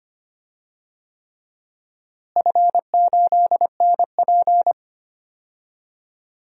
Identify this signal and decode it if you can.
Morse code, heard as a keyed tone, about 25 wpm, 700 Hz. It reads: F8NP
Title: DXCC-100 Call Signs - 4 Characters - Repeated 3x - 25wpm